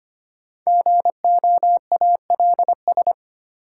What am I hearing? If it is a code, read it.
Morse code, heard as a keyed tone, about 25 wpm, 700 Hz. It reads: GOALS
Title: Sentences from Top 1000 Words - Review Part 6 - New Word Drill - 25wpm